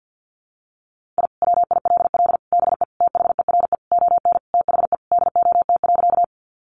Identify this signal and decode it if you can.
Morse code, heard as a keyed tone, about 50 wpm, 700 Hz. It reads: I WILL BE THERE ON THE DOT.
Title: Sentences from Top 1500 Words 50wpm